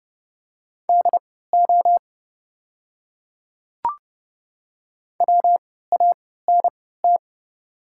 Morse code, heard as a keyed tone, about 30 wpm, spaced as for 17 wpm, 700 Hz. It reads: DO WANT
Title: 3x Character Spacing - Top 200 Words - 30wpm